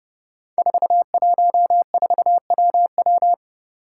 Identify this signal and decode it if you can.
Morse code, heard as a keyed tone, about 30 wpm, 700 Hz. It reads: 414WW
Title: Five Challenging Letter-Number Combinations 30wpm